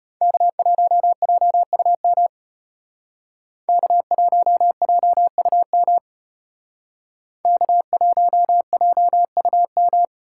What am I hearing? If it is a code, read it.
Morse code, heard as a keyed tone, about 34 wpm, 700 Hz. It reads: K1JUM K1JUM K1JUM